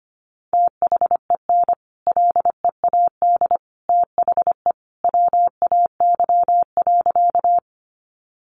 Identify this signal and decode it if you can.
Morse code, heard as a keyed tone, about 25 wpm, 700 Hz. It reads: THEN LEAD THE WAY.
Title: Sentences from Top 500 Words 25wpm